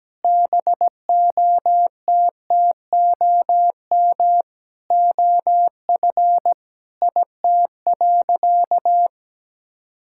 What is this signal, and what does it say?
Morse code, heard as a keyed tone, about 17 wpm, 700 Hz. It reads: BOTTOM OF IT.
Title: Sentences from Top 1000 Words - Review Part 6 - 17wpm